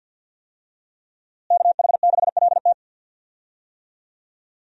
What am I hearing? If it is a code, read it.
Morse code, heard as a keyed tone, about 50 wpm, 700 Hz. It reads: KH6LT
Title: USA Call Signs - 5 Characters - Repeated 2x - 50wpm